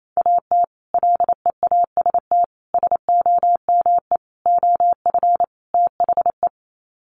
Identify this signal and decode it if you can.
Morse code, heard as a keyed tone, about 28 wpm, 690 Hz. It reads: AT LEAST SOME OF THE
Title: Sets of 5 Words Encore 28wpm